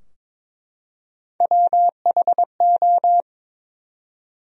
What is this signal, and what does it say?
Morse code, heard as a keyed tone, about 22 wpm, 700 Hz. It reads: WHO
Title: 4x Word-Spacing - Sentences from Top 300 Words - 22wpm